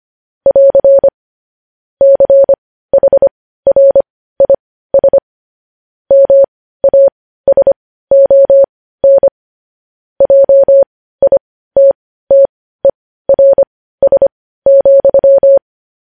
Morse code, heard as a keyed tone, about 25 wpm, spaced as for 15 wpm, 550 Hz. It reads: <AR> CHRIS MASON JITTERS,
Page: news headlines in morse code at 15 wpm